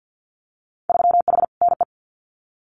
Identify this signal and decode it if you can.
Morse code, heard as a keyed tone, about 50 wpm, 700 Hz. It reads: 3H NE
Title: Winter Field Day Exchanges Encore 50wpm